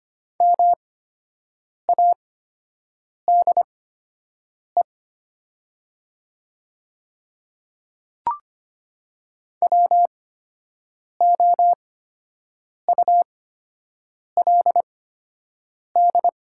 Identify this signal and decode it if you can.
Morse code, heard as a keyed tone, about 25 wpm, spaced as for 7 wpm, 700 Hz. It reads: MADE WOULD
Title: 8x Character Spacing - Top 200 Words - 25wpm